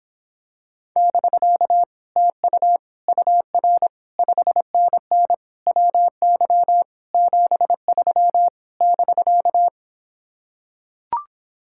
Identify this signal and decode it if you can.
Morse code, heard as a keyed tone, about 26 wpm, 700 Hz. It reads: <BK> TU UR 5NN WY 73 <BK>